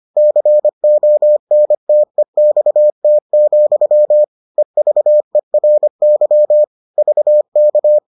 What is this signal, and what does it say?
Morse code, heard as a keyed tone, about 25 wpm, 600 Hz. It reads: CONTEXT, EVERY VK